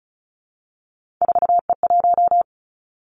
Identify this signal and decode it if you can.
Morse code, heard as a keyed tone, about 36 wpm, 700 Hz. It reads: 4E1
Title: Three Challenging Letter-Number Combinations 35wpm